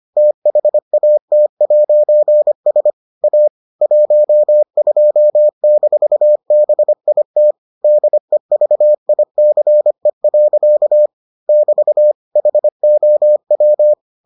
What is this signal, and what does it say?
Morse code, heard as a keyed tone, about 25 wpm, 600 Hz. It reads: THAT'S A 12-BIT DEVICE. <BT> HOW